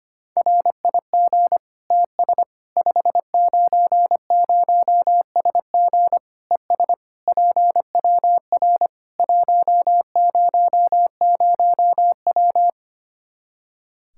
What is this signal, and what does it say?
Morse code, heard as a keyed tone, about 25 wpm, 700 Hz. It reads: RIG TS 590SG ES PWR 100W
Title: Practice QSO - Exchange 2 - Info - 25wpm